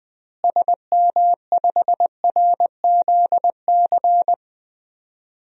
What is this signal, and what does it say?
Morse code, heard as a keyed tone, about 20 wpm, 700 Hz. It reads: SM5RZC
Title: DXCC-100 Call Signs - 6 Characters - Repeated 2x - 20wpm